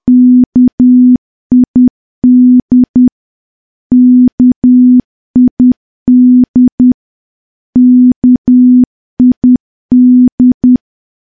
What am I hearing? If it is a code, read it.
Morse code, heard as a keyed tone, about 10 wpm, 250 Hz. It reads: KID KID KID
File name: morse / morse-training-10WPM